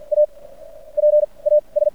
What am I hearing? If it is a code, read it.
Morse code, heard as a keyed tone, about 51 wpm, spaced as for 19 wpm, 600 Hz. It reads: APAS
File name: archived / CRSnet